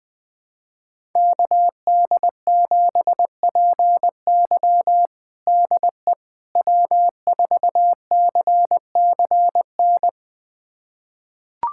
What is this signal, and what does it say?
Morse code, heard as a keyed tone, about 21 wpm, 700 Hz. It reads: KD7PY DE W4CCN
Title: USA Callsign DE Callsign - 5 Characters - 20wpm